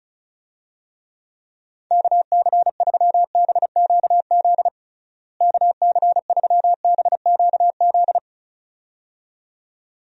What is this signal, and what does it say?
Morse code, heard as a keyed tone, about 36 wpm, 700 Hz. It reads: KC3BQZ KC3BQZ
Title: USA Call Signs - 6 Characters - Repeated 2x - 35wpm